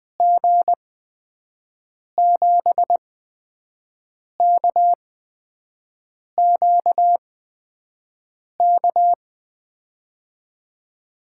Morse code, heard as a keyed tone, about 20 wpm, spaced as for 5 wpm, 700 Hz. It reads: G7KQK
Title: USA Call Signs - 6 Characters - 8x Character Spacing - 20wpm